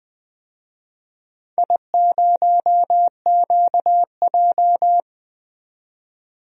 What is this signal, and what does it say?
Morse code, heard as a keyed tone, about 21 wpm, 700 Hz. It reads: I0QJ
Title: DXCC-100 Call Signs - 4 Characters - 20wpm